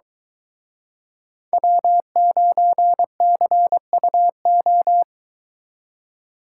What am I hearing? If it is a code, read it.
Morse code, heard as a keyed tone, about 22 wpm, 700 Hz. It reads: W9CUO